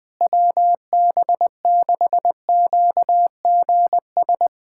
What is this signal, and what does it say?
Morse code, heard as a keyed tone, about 21 wpm, 700 Hz. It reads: WB6QGS
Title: USA Callsign DE Callsign - 6 Characters - 20wpm